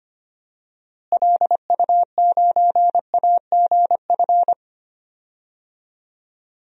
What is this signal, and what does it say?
Morse code, heard as a keyed tone, about 25 wpm, 700 Hz. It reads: LU9AGF
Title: DXCC-100 Call Signs - 6 Characters - Repeated 2x - 25wpm